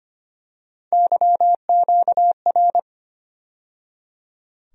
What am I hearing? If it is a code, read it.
Morse code, heard as a keyed tone, about 25 wpm, 700 Hz. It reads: YQR
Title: USA Call Sign Postfixes - 3 Letters 25wpm